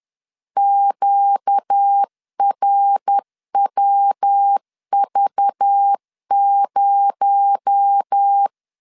Morse code, heard as a keyed tone, about 10 wpm, 800 Hz. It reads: QRWV0
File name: CW / DET / 1 9 1500C S A 10wpm 4511